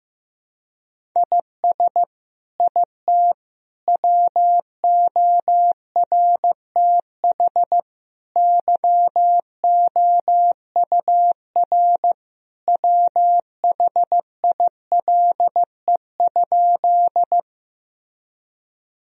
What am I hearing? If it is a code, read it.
Morse code, heard as a keyed tone, about 15 wpm, 700 Hz. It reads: IS IT WORTH YOUR WHILE?
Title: Sentences from Top 1000 Words - Review Part 1 - 15wpm